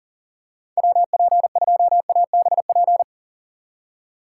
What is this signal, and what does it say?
Morse code, heard as a keyed tone, about 40 wpm, 700 Hz. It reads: WP2ABP
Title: USA Call Signs - 3x Character Spacing - 40wpm